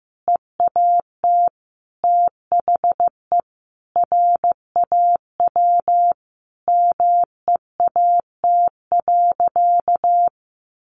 Morse code, heard as a keyed tone, about 15 wpm, 700 Hz. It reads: EAT THE RAW MEAT.
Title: Sentences from Top 1500 Words 15wpm